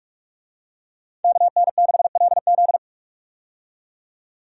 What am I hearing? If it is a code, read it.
Morse code, heard as a keyed tone, about 45 wpm, 700 Hz. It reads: KN6LZ
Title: USA Call Signs - 5 Characters - 45wpm